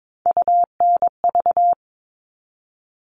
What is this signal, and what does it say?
Morse code, heard as a keyed tone, about 22 wpm, 700 Hz. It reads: UNV
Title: All Cut Numbers - 3-Digit 22wpm